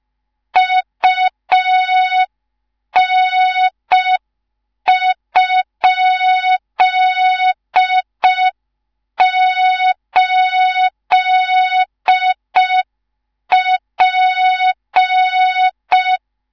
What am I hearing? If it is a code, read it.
Morse code, heard as a keyed tone, about 5 wpm, 3700 Hz. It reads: UN?8P
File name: Morze-HA2MN / hang / 19 GYAKORLAT